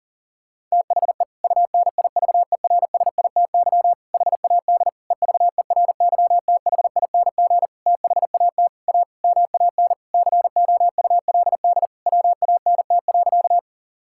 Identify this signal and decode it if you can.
Morse code, heard as a keyed tone, about 40 wpm, 700 Hz. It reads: THE UNIVERSITY HAD EVERYTHING THAT A MAN COULD WANT.
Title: Sentences from Top 1000 Words - Review Part 5 - 40wpm